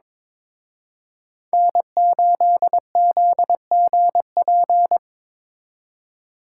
Morse code, heard as a keyed tone, about 26 wpm, 700 Hz. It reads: N8ZGP
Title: Speed Racing - USA 1x3 Call Signs 22wpm